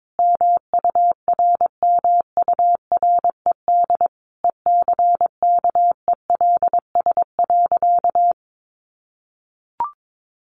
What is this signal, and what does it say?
Morse code, heard as a keyed tone, about 22 wpm, 700 Hz. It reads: MURMURED ECKELS.
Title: A Sound Of Thunder 22wpm